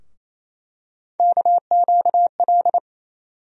Morse code, heard as a keyed tone, about 28 wpm, 700 Hz. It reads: KQL
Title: USA Call Sign Postfixes - 3 Letters 28wpm